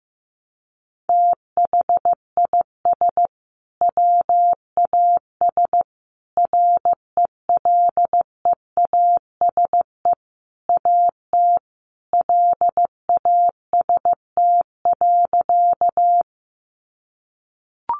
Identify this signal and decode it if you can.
Morse code, heard as a keyed tone, about 15 wpm, 700 Hz. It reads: THIS WAS RELEASE AT LAST.
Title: Sentences from Top 1100 Words 15wpm